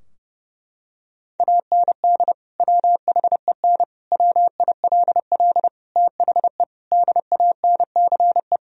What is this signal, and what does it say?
Morse code, heard as a keyed tone, about 30 wpm, 700 Hz. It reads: AND WHEN WILL THE DANCE
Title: Sentences from Top 1000 Words - Review Part 4 - 30wpm